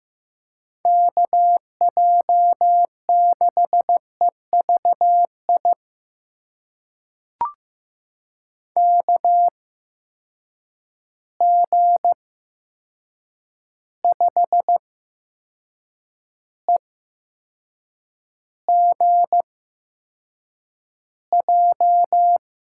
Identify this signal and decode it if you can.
Morse code, heard as a keyed tone, about 15 wpm, spaced as for 6 wpm, 700 Hz. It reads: KJ6EVI KG5EGJ
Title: USA Call Signs - 6 Characters - 8x Character Spacing - 15wpm